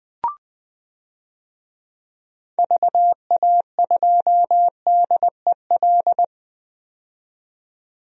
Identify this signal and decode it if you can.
Morse code, heard as a keyed tone, about 20 wpm, 700 Hz. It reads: VA2DEL
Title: DXCC-100 Call Signs - 6 Characters - 20wpm